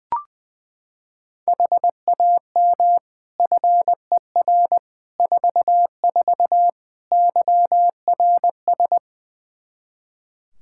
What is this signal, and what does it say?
Morse code, heard as a keyed tone, about 21 wpm, 700 Hz. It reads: HAM FER 44 YRS
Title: Practice QSO - Exchange 3 - More Info - 20wpm